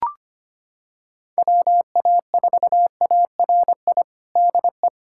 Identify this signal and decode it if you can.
Morse code, heard as a keyed tone, about 25 wpm, 700 Hz. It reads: WA4ARI DE
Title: USA Callsign DE Callsign 25wpm